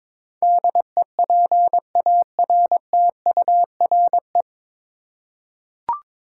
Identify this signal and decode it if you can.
Morse code, heard as a keyed tone, about 22 wpm, 700 Hz. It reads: DEPARTURE
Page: Morse Code Ninja practice 22wpm